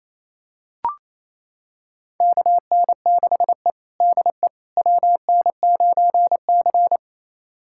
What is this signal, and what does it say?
Morse code, heard as a keyed tone, about 28 wpm, 700 Hz. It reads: KN6E DE WN9C
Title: USA Callsign DE Callsign - 4 Characters - 28wpm